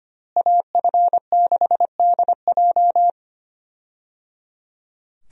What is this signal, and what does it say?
Morse code, heard as a keyed tone, about 25 wpm, 700 Hz. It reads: AF6DJ
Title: USA Call Signs - 5 Characters - Repeated 3x - 25wpm